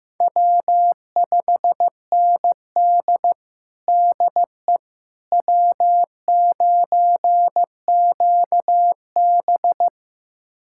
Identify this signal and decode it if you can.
Morse code, heard as a keyed tone, about 15 wpm, 700 Hz. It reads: W5ND DE W9QB
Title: USA Callsign DE Callsign - 4 Characters - 15wpm